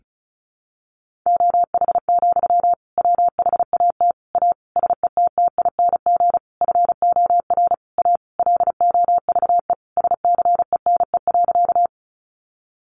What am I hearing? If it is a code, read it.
Morse code, heard as a keyed tone, about 34 wpm, 700 Hz. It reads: OH, WHAT A SETTING FOR A LOVE SCENE.
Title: Sentences from Top 1300 Words 35wpm